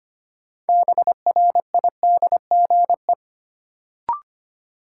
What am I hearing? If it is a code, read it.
Morse code, heard as a keyed tone, about 25 wpm, 700 Hz. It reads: BRIDGE
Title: Sentences from Top 1200 Words - New Word Drill Alt - 25wpm